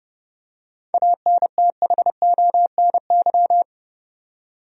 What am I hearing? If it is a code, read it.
Morse code, heard as a keyed tone, about 30 wpm, 700 Hz. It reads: ANTHONY
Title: USA Common Names - 30wpm